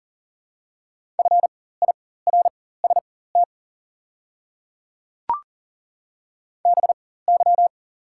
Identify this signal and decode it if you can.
Morse code, heard as a keyed tone, about 40 wpm, spaced as for 19 wpm, 700 Hz. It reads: FIRST BY
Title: 4x Character Spacing - Top 500 Words - 40wpm